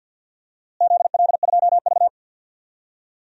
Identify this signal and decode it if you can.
Morse code, heard as a keyed tone, about 51 wpm, 700 Hz. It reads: ZL2V
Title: DXCC-100 Call Signs - 4 Characters - Repeated 2x - 50wpm